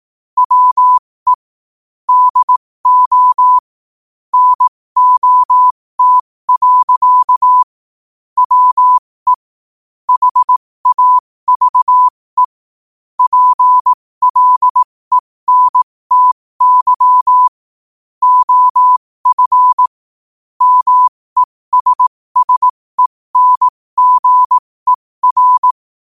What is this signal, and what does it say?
Morse code, heard as a keyed tone, about 18 wpm, spaced as for 15 wpm, 1000 Hz. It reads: WE DO NOT. WE HAVE PLENTY OF MESSENGER
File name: qotdfiles / 15wpm